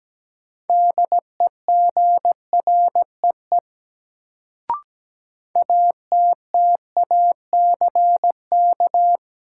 Morse code, heard as a keyed tone, about 17 wpm, 700 Hz. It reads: DEGREE ATTACK
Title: Sentences from Top 1500 Words - New Word Drill - 17wpm